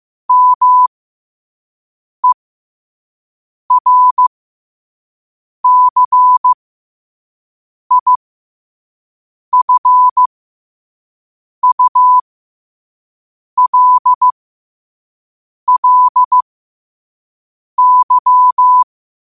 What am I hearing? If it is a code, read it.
Morse code, heard as a keyed tone, about 15 wpm, spaced as for 5 wpm, 1000 Hz. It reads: MERCIFULLY
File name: qotdfiles / 5wpm